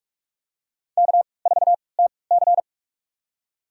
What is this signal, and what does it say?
Morse code, heard as a keyed tone, about 44 wpm, spaced as for 25 wpm, 690 Hz. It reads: K4TC
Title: USA Call Signs - 4 Characters - 3x Character Spacing - 45wpm